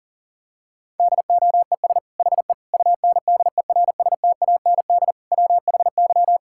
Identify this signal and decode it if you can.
Morse code, heard as a keyed tone, about 40 wpm, 700 Hz. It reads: DOES HE UNDERSTAND WHY
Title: Sentences from Top 1000 Words - Review Part 5 - 40wpm